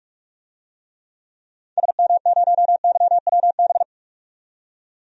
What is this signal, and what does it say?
Morse code, heard as a keyed tone, about 46 wpm, 700 Hz. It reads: SM0YWB
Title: DXCC-100 Call Signs - 6 Characters - 45wpm